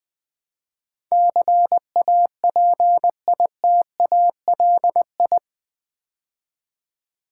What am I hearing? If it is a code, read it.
Morse code, heard as a keyed tone, about 20 wpm, 700 Hz. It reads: CAPITALI
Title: Build A Word - 4 Letters At A Time - 10-Letter Words - 20wpm